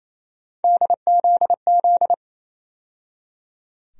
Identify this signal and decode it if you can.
Morse code, heard as a keyed tone, about 28 wpm, 700 Hz. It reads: DZZ
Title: USA Call Sign Postfixes - 3 Letters 28wpm